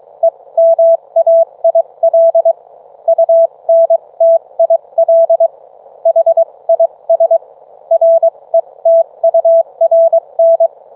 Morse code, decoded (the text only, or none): EMAIL UNTIL HIS RETURN